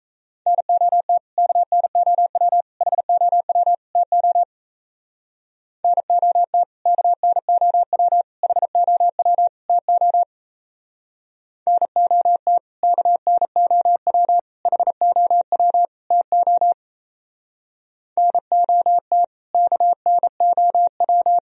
Morse code, heard as a keyed tone, about 35 wpm, 700 Hz. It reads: NOT KNOW HOW TO NOT KNOW HOW TO NOT KNOW HOW TO NOT KNOW
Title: Speed-Racing - Sets of 4 Words 28wpm